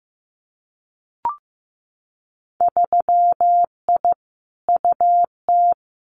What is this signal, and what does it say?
Morse code, heard as a keyed tone, about 15 wpm, 700 Hz. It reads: 3I UT
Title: Winter Field Day Exchanges Encore 15wpm